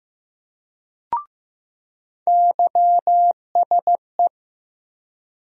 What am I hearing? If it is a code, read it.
Morse code, heard as a keyed tone, about 15 wpm, 700 Hz. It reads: YSE
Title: USA Call Sign Postfixes - 3 Letters 15wpm